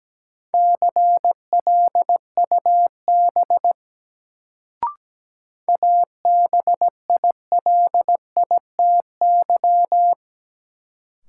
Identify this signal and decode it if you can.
Morse code, heard as a keyed tone, about 17 wpm, 700 Hz. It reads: CLUB ABILITY
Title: Sentences from Top 1200 Words - New Word Drill Alt - 17wpm